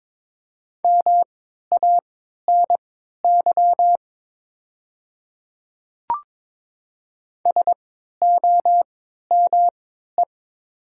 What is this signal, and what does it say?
Morse code, heard as a keyed tone, about 22 wpm, spaced as for 13 wpm, 700 Hz. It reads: MANY SOME
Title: 3x Character Spacing - Top 300 Words - 22wpm